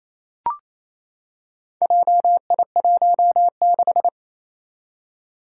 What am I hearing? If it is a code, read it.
Morse code, heard as a keyed tone, about 28 wpm, 700 Hz. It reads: JI16
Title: Four Challenging Letter-Number Combinations 28wpm